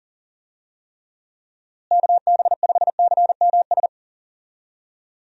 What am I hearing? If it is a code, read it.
Morse code, heard as a keyed tone, about 37 wpm, 700 Hz. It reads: KB5CMS
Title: USA Call Signs - 6 Characters - Repeated 3x - 40wpm